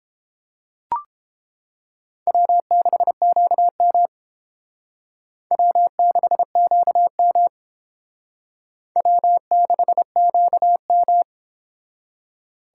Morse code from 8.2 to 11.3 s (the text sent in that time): W6QM